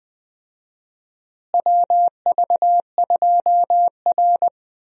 Morse code, heard as a keyed tone, about 20 wpm, 700 Hz. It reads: WV2R
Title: USA Call Signs - 4 Characters - Repeated 2x - 20wpm